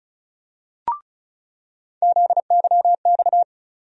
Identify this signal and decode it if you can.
Morse code, heard as a keyed tone, about 36 wpm, 700 Hz. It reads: ZYX
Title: USA Call Sign Postfixes - 3 Letters 35wpm